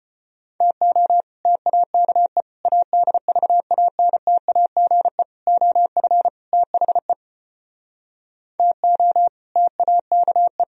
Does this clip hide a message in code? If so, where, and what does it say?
Yes: Morse code from 0.6 to 10.6 s, heard as a keyed tone, about 33 wpm, 700 Hz. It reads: TO TAKE ADVANTAGE OF THE TO TAKE